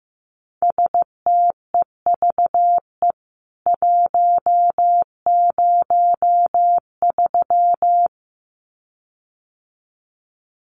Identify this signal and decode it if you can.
Morse code, heard as a keyed tone, about 15 wpm, 700 Hz. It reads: STEVE 103
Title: CWT Exchange 15wpm